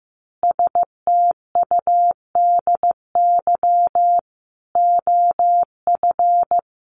Morse code, heard as a keyed tone, about 15 wpm, 700 Hz. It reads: STUDY OF
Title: Sentences from Top 900 Words 15wpm